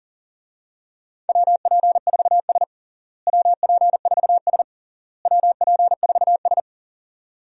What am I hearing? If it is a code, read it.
Morse code, heard as a keyed tone, about 37 wpm, 700 Hz. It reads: WP4S WP4S WP4S
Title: USA Call Signs - 4 Characters - Repeated 3x - 40wpm